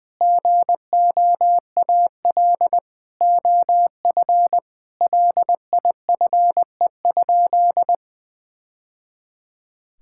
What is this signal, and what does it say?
Morse code, heard as a keyed tone, about 20 wpm, 700 Hz. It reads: GOAL OF LIFE?
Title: Sentences from Top 1000 Words - Review Part 5 - 20wpm